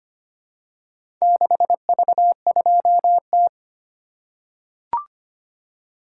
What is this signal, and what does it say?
Morse code, heard as a keyed tone, about 25 wpm, 700 Hz. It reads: 6V2T
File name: practice / 25wpm / Four Letter-Number Combinations - 25wpm